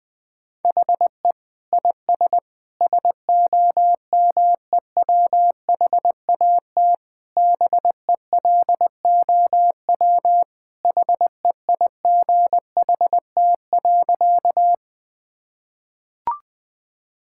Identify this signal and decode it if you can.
Morse code, heard as a keyed tone, about 20 wpm, 700 Hz. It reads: HE IS SOMEWHAT BELOW HEIGHT.
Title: Sentences from Top 1600 Words 20wpm